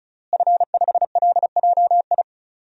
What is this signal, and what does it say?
Morse code, heard as a keyed tone, about 34 wpm, 700 Hz. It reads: F5LJI
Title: DXCC-100 Call Signs - 5 Characters - Repeated 3x - 35wpm